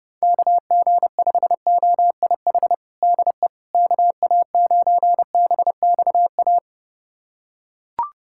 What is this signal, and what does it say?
Morse code, heard as a keyed tone, about 32 wpm, 700 Hz. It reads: KG5OIH DE KA9BXA
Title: USA Callsign DE Callsign - 6 Characters - 30wpm